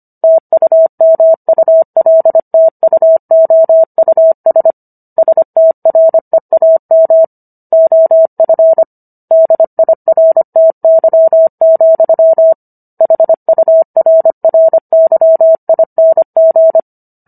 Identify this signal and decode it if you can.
Morse code, heard as a keyed tone, about 25 wpm, 650 Hz. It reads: TUMULTUOUS STREAM OF DIRTY, HURRYING